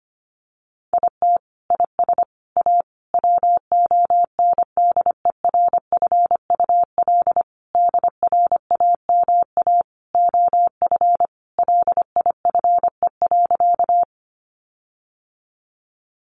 Morse code, heard as a keyed tone, about 25 wpm, 700 Hz. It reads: IT IS A WONDERFUL DRAMA OF LIFE.
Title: Sentences from Top 1500 Words 25wpm